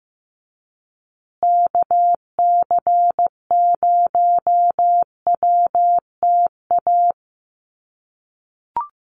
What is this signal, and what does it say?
Morse code, heard as a keyed tone, about 15 wpm, 700 Hz. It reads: KC0WTA
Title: USA Call Signs Repeated 3x 15wpm